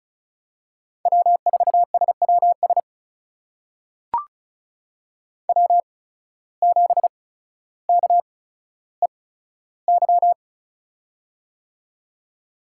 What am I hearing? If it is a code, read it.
Morse code, heard as a keyed tone, about 36 wpm, spaced as for 15 wpm, 700 Hz. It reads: W4SWS W7KEY
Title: USA Call Signs - 5 Characters - 8x Character Spacing - 35wpm